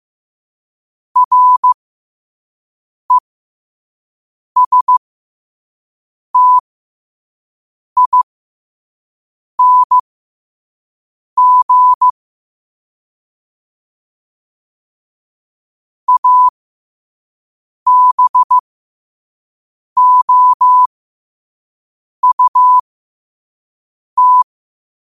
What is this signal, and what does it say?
Morse code, heard as a keyed tone, about 15 wpm, spaced as for 5 wpm, 1000 Hz. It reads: RESTING ABOUT
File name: qotdfiles / 5wpm